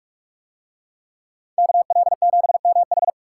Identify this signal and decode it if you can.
Morse code, heard as a keyed tone, about 40 wpm, 700 Hz. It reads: KL7MH